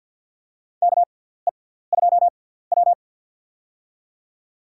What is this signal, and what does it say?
Morse code, heard as a keyed tone, about 52 wpm, spaced as for 17 wpm, 700 Hz. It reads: KE2W
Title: USA Call Signs - 4 Characters - 6x Character Spacing - 50wpm